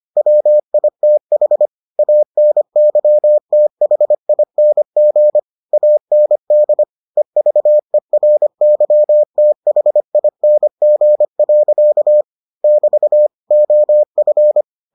Morse code, heard as a keyed tone, about 25 wpm, 600 Hz. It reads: WITH ANYTHING AND EVERYTHING. <BT> OF